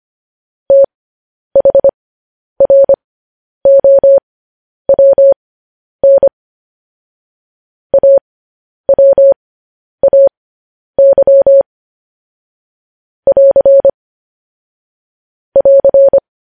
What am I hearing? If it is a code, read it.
Morse code, heard as a keyed tone, about 25 wpm, spaced as for 10 wpm, 550 Hz. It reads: THROWN AWAY <AR> <AR>
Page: news headlines in morse code at 10 wpm